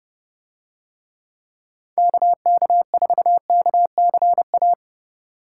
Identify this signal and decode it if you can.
Morse code, heard as a keyed tone, about 28 wpm, 700 Hz. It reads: KK4KCA